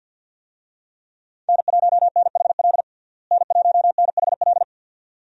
Morse code, heard as a keyed tone, about 52 wpm, 700 Hz. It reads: N1NHL N1NHL
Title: USA Call Signs - 5 Characters - Repeated 2x - 50wpm